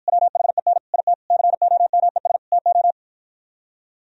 Morse code, heard as a keyed tone, about 61 wpm, 700 Hz. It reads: WHEN IT COMES TO